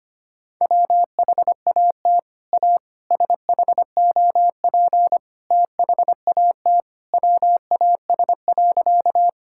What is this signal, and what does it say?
Morse code, heard as a keyed tone, about 25 wpm, 700 Hz. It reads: WHAT A SHOP THAT WAS.
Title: Sentences from Top 1600 Words 25wpm